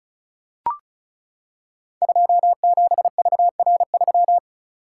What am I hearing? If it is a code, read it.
Morse code, heard as a keyed tone, about 36 wpm, 700 Hz. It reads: 27VR3
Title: Code Groups - 5 Letter:Numbers - 35wpm